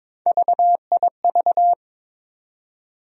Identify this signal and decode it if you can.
Morse code, heard as a keyed tone, about 22 wpm, 700 Hz. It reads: VIV
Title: USA Call Sign Postfixes - 3 Letters 22wpm